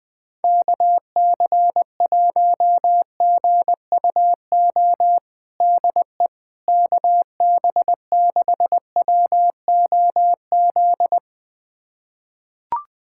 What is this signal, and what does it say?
Morse code, heard as a keyed tone, about 21 wpm, 700 Hz. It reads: KC1GUO DE KB6WOZ
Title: USA Callsign DE Callsign 20wpm